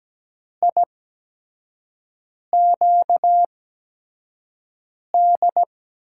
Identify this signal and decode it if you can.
Morse code, heard as a keyed tone, about 17 wpm, spaced as for 5 wpm, 700 Hz. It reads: IQD
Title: USA Call Signs - 6 Characters - 8x Character Spacing - 17wpm